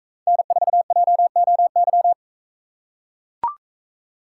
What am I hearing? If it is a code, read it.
Morse code, heard as a keyed tone, about 40 wpm, 700 Hz. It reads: N4JOY